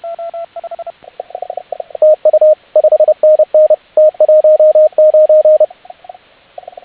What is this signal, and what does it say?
Morse code, heard as a keyed tone, about 30 wpm, 620 Hz, with radio noise. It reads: TU 5NN T19